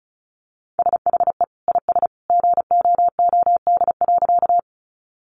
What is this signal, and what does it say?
Morse code, heard as a keyed tone, about 34 wpm, 700 Hz. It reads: SHE IS GOOD.
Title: Easy Phrases 35wpm